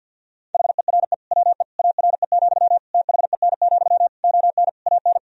Morse code, heard as a keyed tone, about 50 wpm, 700 Hz. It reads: HERE WE ARE, THEN, ON AN
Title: Sentences from Top 1000 Words - Review Part 3 - 50wpm